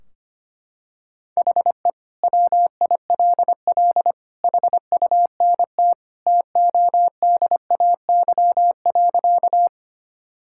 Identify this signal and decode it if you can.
Morse code, heard as a keyed tone, about 25 wpm, 700 Hz. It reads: HE WILL HUNT TODAY.
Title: Sentences from Top 1600 Words 25wpm